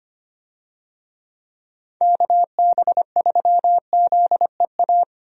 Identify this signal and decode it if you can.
Morse code, heard as a keyed tone, about 25 wpm, 700 Hz. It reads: KB3ZEA
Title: USA Call Signs - 6 Characters - 25wpm